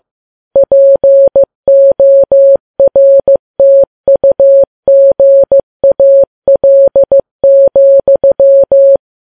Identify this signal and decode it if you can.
Morse code, heard as a keyed tone, about 15 wpm, 550 Hz. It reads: PORTUGAL,
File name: Files Web tls / Morse on9fv / morse15kps